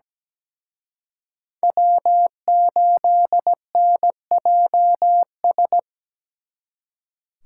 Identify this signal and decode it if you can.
Morse code, heard as a keyed tone, about 20 wpm, 700 Hz. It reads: W8NJS
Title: Speed Racing - USA 1x3 Call Signs 17wpm